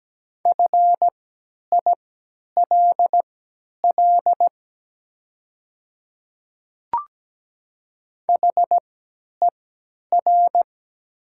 Morse code, heard as a keyed tone, about 17 wpm, spaced as for 10 wpm, 700 Hz. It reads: FILL HER
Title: 3x Character Spacing - Top 500 Words - 17wpm